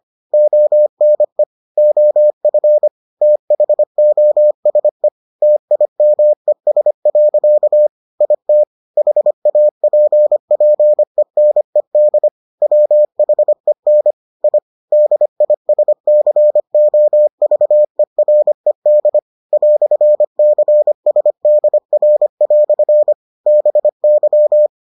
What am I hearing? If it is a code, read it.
Morse code, heard as a keyed tone, about 25 wpm, 600 Hz. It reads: ONE OF THOSE TIMES. IT HAPPENED WHEN I DISCOVERED "CSDR" BY